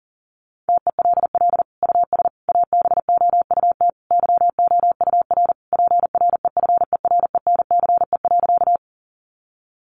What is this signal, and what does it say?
Morse code, heard as a keyed tone, about 40 wpm, 700 Hz. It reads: TELL US ABOUT YOUR PREFERENCE.
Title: Sentences from Top 1500 Words 40wpm